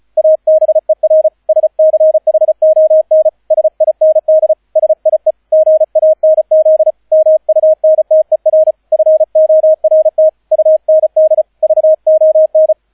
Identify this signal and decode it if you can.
Morse code, heard as a keyed tone, about 35 wpm, 630 Hz, with radio noise. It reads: ABER SCHON SIND SIE GANZ MUNTER FORT UND VON